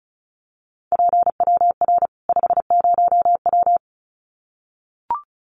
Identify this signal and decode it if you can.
Morse code, heard as a keyed tone, about 34 wpm, 700 Hz. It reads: PWR 50W
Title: Common QSO Phrases 35wpm